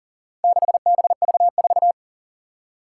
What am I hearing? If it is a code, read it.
Morse code, heard as a keyed tone, about 40 wpm, 700 Hz. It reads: 6BV4
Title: Four Challenging Letter-Number Combinations 40wpm